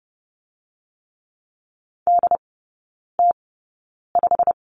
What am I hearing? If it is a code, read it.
Morse code, heard as a keyed tone, about 30 wpm, spaced as for 9 wpm, 690 Hz. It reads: DT5
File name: Lesson 17e - Letter u - Three Characters 30wpm